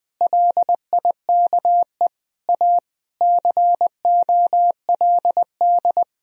Morse code, heard as a keyed tone, about 20 wpm, 700 Hz. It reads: LIKE A COLD
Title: Sentences from Top 1000 Words - Review Part 2 - 20wpm